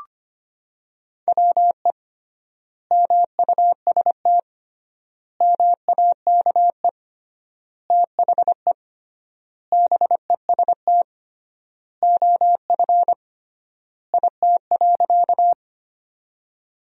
Morse code, heard as a keyed tone, about 25 wpm, 700 Hz. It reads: WE MUST MAKE THE BEST OF IT.
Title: 3x Word-Spacing - Sentences from Top 400 Words - 25wpm